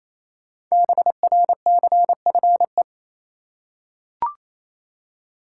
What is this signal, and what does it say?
Morse code, heard as a keyed tone, about 28 wpm, 700 Hz. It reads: BRCFE
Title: Code Groups - 5 Letters - 28wpm